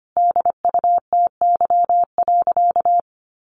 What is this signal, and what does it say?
Morse code, heard as a keyed tone, about 25 wpm, 700 Hz. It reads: DUTY.
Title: Sentences from Top 1500 Words 25wpm